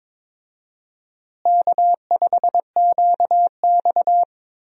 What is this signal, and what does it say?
Morse code, heard as a keyed tone, about 22 wpm, 700 Hz. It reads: K5QX
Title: USA Call Signs - 4 Characters - Repeated 3x - 22wpm